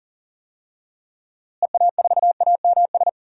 Morse code, heard as a keyed tone, about 40 wpm, 700 Hz. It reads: EA4AMS
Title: DXCC-100 Call Signs - 6 Characters - Repeated 3x - 40wpm